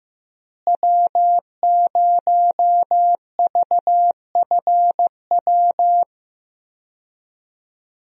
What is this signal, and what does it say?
Morse code, heard as a keyed tone, about 15 wpm, 700 Hz. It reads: W0VFW
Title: USA Call Signs - 5 Characters - Repeated 2x - 15wpm